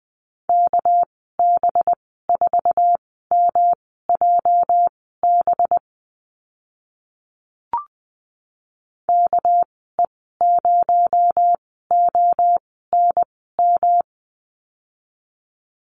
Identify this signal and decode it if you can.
Morse code, heard as a keyed tone, about 20 wpm, spaced as for 14 wpm, 700 Hz. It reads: KB4MJB KE0ONM
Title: 245 Lesson 39g - Letter z - Call Signs 20wpm